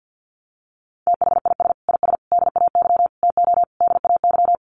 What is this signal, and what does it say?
Morse code, heard as a keyed tone, about 50 wpm, 700 Hz. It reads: THIS IS DAY TO DAY
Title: Sets of 5 Words Encore 50wpm